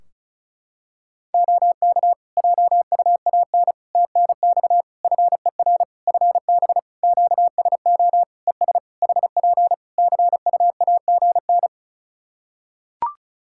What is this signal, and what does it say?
Morse code, heard as a keyed tone, about 36 wpm, 700 Hz. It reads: OK JUAN TNX FER FB QSO ES HP CUAGN
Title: K1USN SST QSO Practice 35wpm